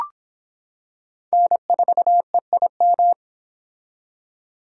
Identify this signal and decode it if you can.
Morse code, heard as a keyed tone, about 25 wpm, 700 Hz. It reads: N4EIM